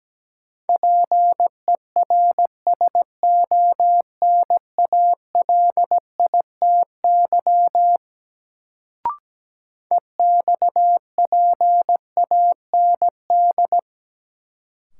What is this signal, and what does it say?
Morse code, heard as a keyed tone, about 17 wpm, 700 Hz. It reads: PERSONALITY EXPAND
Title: Sentences from Top 1400 Words - New Word Drill Alt - 17wpm